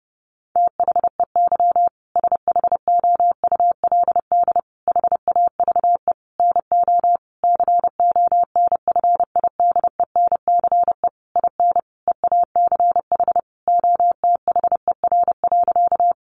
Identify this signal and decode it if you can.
Morse code, heard as a keyed tone, about 30 wpm, 700 Hz. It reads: THEY SHOULD HAVE NO CONFIDENCE IN EACH OTHER.
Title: Sentences from Top 1500 Words 30wpm